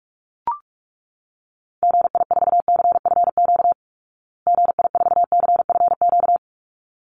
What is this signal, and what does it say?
Morse code, heard as a keyed tone, about 44 wpm, 700 Hz. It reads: GI4CFQ GI4CFQ
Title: Call Signs Repeated 3x - 45wpm